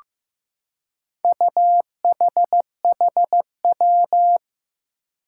Morse code, heard as a keyed tone, about 15 wpm, 700 Hz. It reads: UHHW
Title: Four Challenging Letter-Number Combinations 15wpm